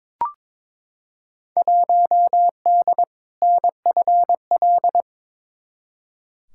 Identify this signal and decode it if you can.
Morse code, heard as a keyed tone, about 22 wpm, 700 Hz. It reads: 1D NFL
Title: Field Day Exchanges - 22wpm